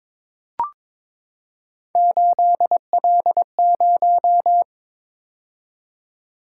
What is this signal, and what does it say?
Morse code, heard as a keyed tone, about 22 wpm, 700 Hz. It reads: 8L0
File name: practice / 22wpm / Three Letter-Number Combinations - 22wpm